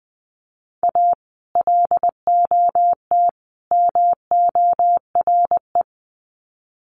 Morse code, heard as a keyed tone, about 20 wpm, 700 Hz. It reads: A LOT MORE
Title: Sets of 3 Words 20wpm